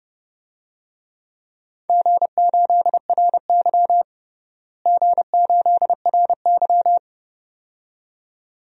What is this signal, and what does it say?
Morse code, heard as a keyed tone, about 32 wpm, 700 Hz. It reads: G8RY G8RY
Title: DXCC-100 Call Signs - 4 Characters - Repeated 2x - 30wpm